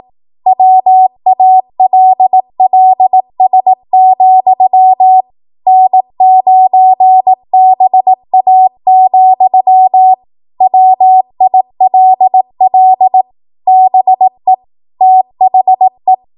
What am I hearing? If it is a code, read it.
Morse code, heard as a keyed tone, about 18 wpm, 750 Hz. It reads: WALLS, N9BA, WILL BE THE